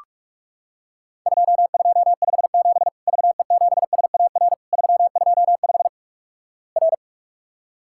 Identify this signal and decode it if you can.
Morse code, heard as a keyed tone, about 46 wpm, 700 Hz. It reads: 2257 VE7SAR 325 R
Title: Worked All Europe DX Contest - QTC Exchange - 45wpm